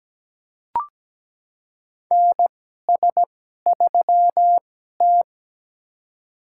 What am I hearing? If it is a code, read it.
Morse code, heard as a keyed tone, about 17 wpm, spaced as for 12 wpm, 700 Hz. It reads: NS3T
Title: USA Call Signs - 4 Characters - 2x Character Spacing - 17wpm